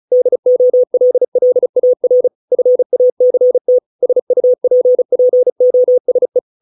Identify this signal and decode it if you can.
Morse code, heard as a keyed tone, about 34 wpm, 500 Hz. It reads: DOLLAR FACT SUPPOSE